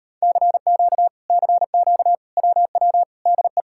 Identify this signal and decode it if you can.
Morse code, heard as a keyed tone, about 38 wpm, 700 Hz. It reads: CQ CQ WW DE